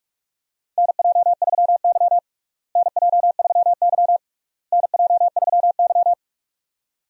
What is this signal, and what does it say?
Morse code, heard as a keyed tone, about 46 wpm, 700 Hz. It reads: NJ3Y NJ3Y NJ3Y
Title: USA Call Signs - 4 Characters - Repeated 3x - 45wpm